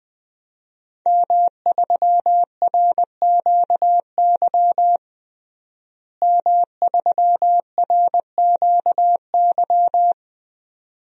Morse code, heard as a keyed tone, about 21 wpm, 700 Hz. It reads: M3RQY M3RQY
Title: DXCC-100 Call Signs - 5 Characters - Repeated 2x - 20wpm